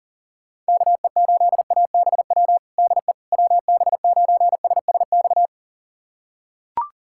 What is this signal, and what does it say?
Morse code, heard as a keyed tone, about 37 wpm, 700 Hz. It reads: KE8ABW DE WB9SSX
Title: USA Callsign DE Callsign - 6 Characters - 40wpm